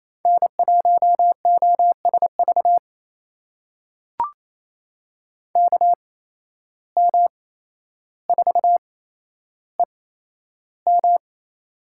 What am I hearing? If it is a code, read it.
Morse code, heard as a keyed tone, about 28 wpm, spaced as for 12 wpm, 700 Hz. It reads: N1OSV KM4EM
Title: USA Call Signs - 5 Characters - 8x Character Spacing - 28wpm